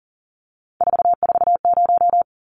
Morse code, heard as a keyed tone, about 37 wpm, 700 Hz. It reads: 440
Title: Numbers - 3-Digit 40wpm